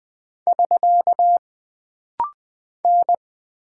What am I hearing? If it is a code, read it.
Morse code, heard as a keyed tone, about 20 wpm, 700 Hz. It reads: <SK> N
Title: Single Character (Letter:Number:Punctuation:Prosign) - Rapid-Fire - 20wpm